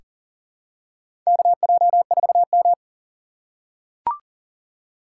Morse code, heard as a keyed tone, about 40 wpm, 700 Hz. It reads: KJ4M
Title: USA Call Signs - 4 Characters - 4x Character Spacing - 40wpm